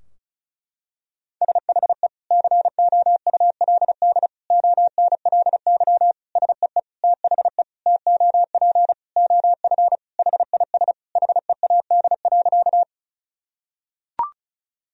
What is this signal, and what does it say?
Morse code, heard as a keyed tone, about 36 wpm, 700 Hz. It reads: SHE COULD ONLY SEE THE TOP OF HIS HEAD.
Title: Sentences from Top 1000 Words - Review Part 2 - 35wpm